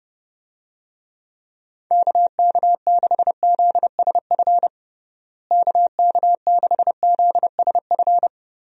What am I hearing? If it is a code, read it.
Morse code, heard as a keyed tone, about 30 wpm, 700 Hz. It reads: KK6ZSF KK6ZSF
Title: USA Call Signs - 6 Characters - Repeated 2x - 30wpm